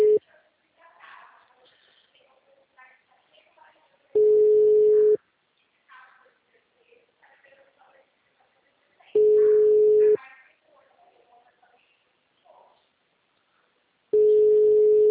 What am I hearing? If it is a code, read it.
Ringback tone (Europe, 425 Hz), with speech.
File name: out- -2008- - - .6